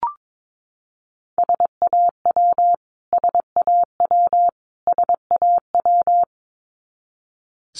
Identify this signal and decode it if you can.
Morse code, heard as a keyed tone, about 22 wpm, 700 Hz. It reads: SAW SAW SAW
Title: Top 300 Words Repeated 22wpm